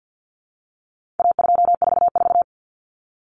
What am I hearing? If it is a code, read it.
Morse code, heard as a keyed tone, about 48 wpm, 700 Hz. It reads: A244
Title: Common Cut Numbers - 4-Digit 50wpm